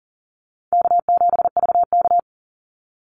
Morse code, heard as a keyed tone, about 39 wpm, 700 Hz. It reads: K7VK
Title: NA SOTA Chaser Callsigns 40wpm